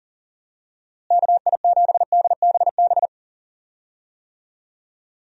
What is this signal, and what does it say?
Morse code, heard as a keyed tone, about 40 wpm, 700 Hz. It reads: KI7DBB
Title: USA Call Signs - 6 Characters - Repeated 2x - 40wpm